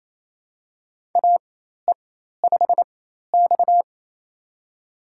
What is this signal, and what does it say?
Morse code, heard as a keyed tone, about 28 wpm, spaced as for 13 wpm, 700 Hz. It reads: AE5X
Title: USA Call Signs - 4 Characters - 4x Character Spacing - 28wpm